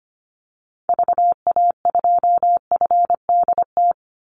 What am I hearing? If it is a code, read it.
Morse code, heard as a keyed tone, about 25 wpm, 700 Hz. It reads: VA2FDT
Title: Call Signs - 25wpm